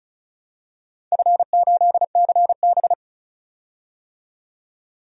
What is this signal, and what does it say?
Morse code, heard as a keyed tone, about 34 wpm, 700 Hz. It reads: F8CB
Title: DXCC-100 Call Signs - 4 Characters - 35wpm